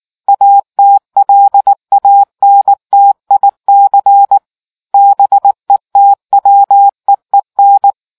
Morse code, heard as a keyed tone, about 18 wpm, 800 Hz. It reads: ATLANTIC BETWEEN